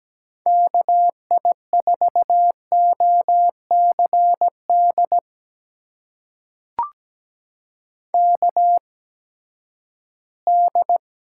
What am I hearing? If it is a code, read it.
Morse code, heard as a keyed tone, about 17 wpm, spaced as for 12 wpm, 700 Hz. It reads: KI4OCD KD
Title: USA Call Signs - 8x Character Spacing - 17wpm